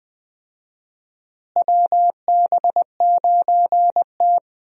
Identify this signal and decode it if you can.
Morse code, heard as a keyed tone, about 21 wpm, 700 Hz. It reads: WB9T
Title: USA Call Signs - 4 Characters - Repeated 2x - 20wpm